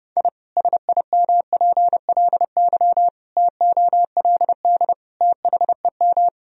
Morse code, heard as a keyed tone, about 28 wpm, 700 Hz. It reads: I SIMPLY TOLD THEM
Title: Sentences from Top 1000 Words - Review Part 4 - 30wpm